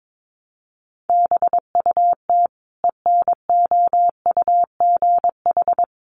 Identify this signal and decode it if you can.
Morse code, heard as a keyed tone, about 22 wpm, 700 Hz. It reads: BUT ENOUGH
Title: Sentences from Top 400 Words 22wpm